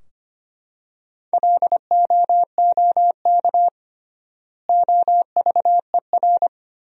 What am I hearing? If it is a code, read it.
Morse code, heard as a keyed tone, about 25 wpm, 700 Hz. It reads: LOOK OVER
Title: 3x Word-Spacing - Sentences from Top 100 Words - 25wpm